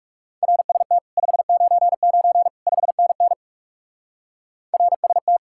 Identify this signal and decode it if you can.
Morse code, heard as a keyed tone, about 39 wpm, 700 Hz. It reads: RST 599 5NN RST